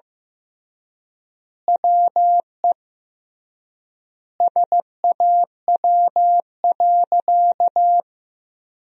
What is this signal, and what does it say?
Morse code, heard as a keyed tone, about 16 wpm, 700 Hz. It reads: WE SAW.
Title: 3x Word-Spacing - Sentences from Top 300 Words - 15wpm